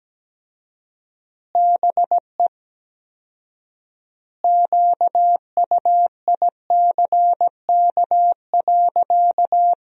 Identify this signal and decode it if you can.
Morse code, heard as a keyed tone, about 17 wpm, 700 Hz. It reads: BE QUICK.
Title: 4x Word-Spacing - Sentences from Top 500 Words - 17wpm